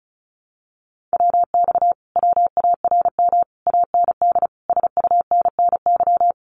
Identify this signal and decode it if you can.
Morse code, heard as a keyed tone, about 36 wpm, 700 Hz. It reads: WX WARM AND SUNNY
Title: Common QSO Phrases Encore 35wpm